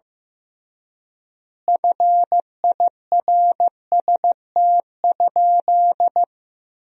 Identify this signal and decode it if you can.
Morse code, heard as a keyed tone, about 15 wpm, 700 Hz. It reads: FIRST?
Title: 3x Word-Spacing - Sentences from Top 100 Words - 15wpm